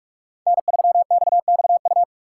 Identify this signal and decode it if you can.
Morse code, heard as a keyed tone, about 58 wpm, 700 Hz. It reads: N3XXU